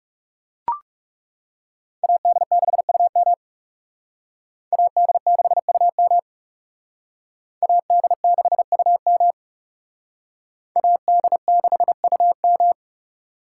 Morse code, heard as a keyed tone, about 38 wpm, 700 Hz. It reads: AD6UM AD6UM AD6UM AD6UM